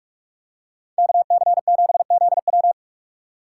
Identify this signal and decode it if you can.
Morse code, heard as a keyed tone, about 42 wpm, 700 Hz. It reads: KC7ZW